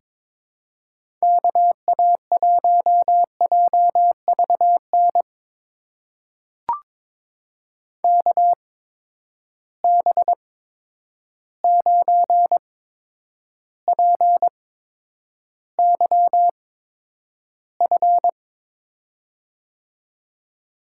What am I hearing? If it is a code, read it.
Morse code, heard as a keyed tone, about 22 wpm, spaced as for 9 wpm, 700 Hz. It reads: KA1JVN KB9PYF
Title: USA Call Signs - 6 Characters - 8x Character Spacing - 22wpm